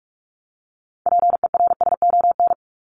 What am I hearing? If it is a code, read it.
Morse code, heard as a keyed tone, about 44 wpm, 700 Hz. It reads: PERSON
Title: Top 500 Words 45wpm